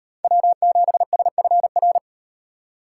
Morse code, heard as a keyed tone, about 38 wpm, 700 Hz. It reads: W7SFR